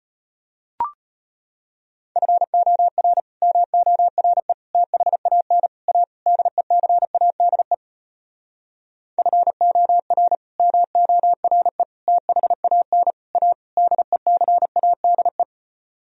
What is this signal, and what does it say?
Morse code, heard as a keyed tone, about 33 wpm, 700 Hz. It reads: FOR MORE THAN A DECADE FOR MORE THAN A DECADE